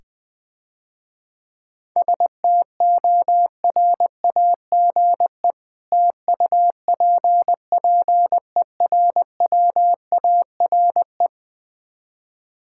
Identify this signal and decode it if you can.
Morse code, heard as a keyed tone, about 21 wpm, 700 Hz. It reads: STORAGE TUPPERWARE
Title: Difficult Words - Speed-Racing with Hints - 20wpm